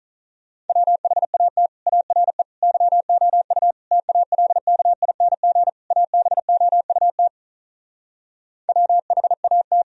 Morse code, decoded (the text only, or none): WHAT ARE YOU TALKING ABOUT WHAT